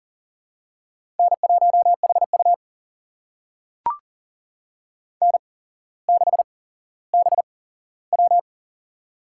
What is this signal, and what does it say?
Morse code, heard as a keyed tone, about 40 wpm, spaced as for 17 wpm, 700 Hz. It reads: N1HU N6BW
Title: USA Call Signs - 4 Characters - 8x Character Spacing - 40wpm